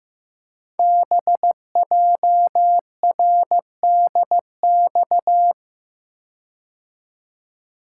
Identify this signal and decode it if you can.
Morse code, heard as a keyed tone, about 15 wpm, 700 Hz. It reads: BJRDX
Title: Code Groups - 5 Letters - 15wpm